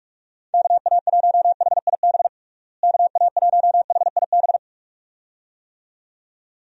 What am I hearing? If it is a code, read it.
Morse code, heard as a keyed tone, about 46 wpm, 700 Hz. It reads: KA1HIB KA1HIB
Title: USA Call Signs - 6 Characters - Repeated 2x - 45wpm